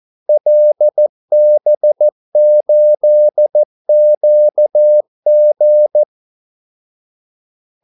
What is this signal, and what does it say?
Morse code, heard as a keyed tone, about 14 wpm, 600 Hz. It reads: LB8QG
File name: Callsigns-14wpm